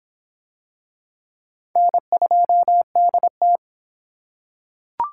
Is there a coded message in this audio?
Yes: Morse code from 1.8 to 3.6 s, heard as a keyed tone, about 23 wpm, 700 Hz. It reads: N2DT